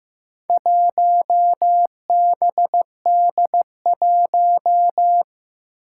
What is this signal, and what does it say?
Morse code, heard as a keyed tone, about 15 wpm, 700 Hz. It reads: 1BD1
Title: Four Challenging Letter-Number Combinations 15wpm